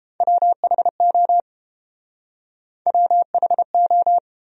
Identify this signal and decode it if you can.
Morse code, heard as a keyed tone, about 30 wpm, 700 Hz. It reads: WHO WHO